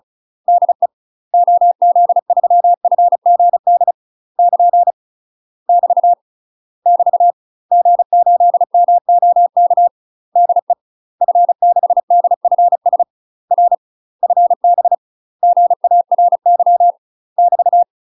Morse code, heard as a keyed tone, about 34 wpm, 700 Hz. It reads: DE OZ3FGD <KN> <BT> <BT> G8MOK DE F6DFS R FB GARY <BT>
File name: UK MASTER QSOs TRAINING / 35WPM / UK MASTER QSOs 0034